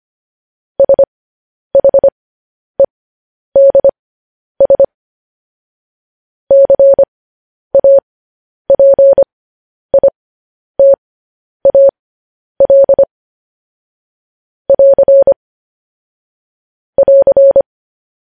Morse code, heard as a keyed tone, about 25 wpm, spaced as for 10 wpm, 550 Hz. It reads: SHEDS CAPITAL <AR> <AR>
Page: news headlines in morse code at 10 wpm